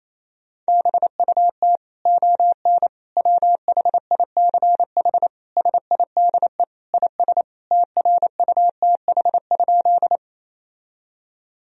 Morse code, heard as a keyed tone, about 28 wpm, 700 Hz. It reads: BUT ON WHICH SIDE IS TRUTH?
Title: Sentences from Top 1000 Words - Review Part 1 - 28wpm